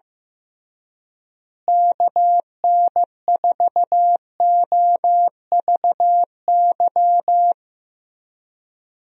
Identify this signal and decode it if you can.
Morse code, heard as a keyed tone, about 16 wpm, 700 Hz. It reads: KN4OVY
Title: USA Call Signs - 6 Characters - Repeated 3x - 15wpm